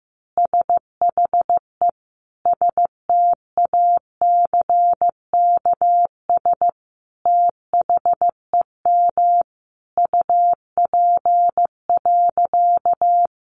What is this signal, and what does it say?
Morse code, heard as a keyed tone, about 15 wpm, 700 Hz. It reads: SHE STACKS THEM UP.
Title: Easy Phrases 15wpm